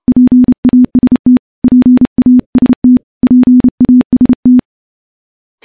Morse code, heard as a keyed tone, about 30 wpm, 250 Hz. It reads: PAST PAST PAST
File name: morse / morse-training-30WPM